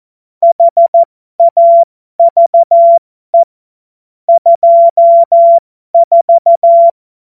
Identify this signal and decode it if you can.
Morse code, heard as a keyed tone, about 14 wpm, spaced as for 12 wpm, 680 Hz. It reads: HAVE 24
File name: Morsekurs / Witze